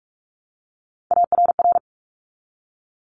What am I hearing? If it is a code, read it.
Morse code, heard as a keyed tone, about 46 wpm, 700 Hz. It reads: ARR
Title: Top Three Letter Combinations in Words 45wpm